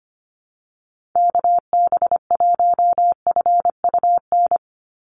Morse code, heard as a keyed tone, about 25 wpm, 700 Hz. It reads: KB1FUN
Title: USA Call Signs 25wpm